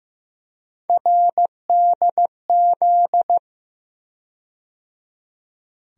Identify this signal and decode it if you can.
Morse code, heard as a keyed tone, about 15 wpm, 700 Hz. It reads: RDZ
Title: USA Call Sign Postfixes - 3 Letters 15wpm